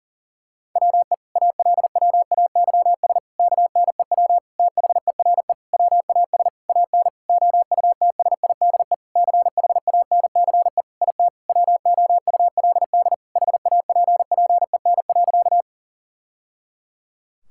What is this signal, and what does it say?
Morse code, heard as a keyed tone, about 40 wpm, 700 Hz. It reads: WE ALWAYS KNEW THERE WAS AN OUTSIDE CHANCE IT WOULD HAPPEN.
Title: Sentences from Top 1000 Words - Review Part 6 - 40wpm